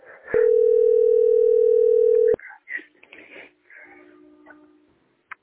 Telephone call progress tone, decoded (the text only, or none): ringback tone